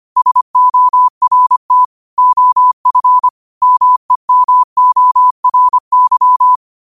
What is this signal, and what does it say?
Morse code, heard as a keyed tone, about 25 wpm, 1000 Hz. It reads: SORT OF MEMORY